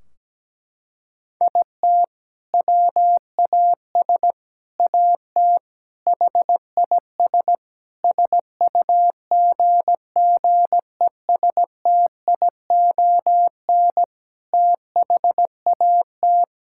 Morse code, heard as a keyed tone, about 17 wpm, 700 Hz. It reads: IT WAS AT HIS SUGGESTION THAT
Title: Sentences from Top 1600 Words 17wpm